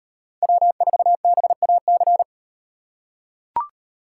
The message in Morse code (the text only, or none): W4BAC